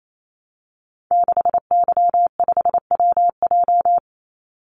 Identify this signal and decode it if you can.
Morse code, heard as a keyed tone, about 28 wpm, 700 Hz. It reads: 6Y5WJ
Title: Call Signs - 28wpm